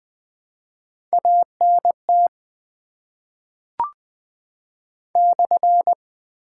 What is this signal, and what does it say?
Morse code, heard as a keyed tone, about 20 wpm, 700 Hz. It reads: ANT /
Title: Top 90 QSO Elements - 8x Character Spacing - 20wpm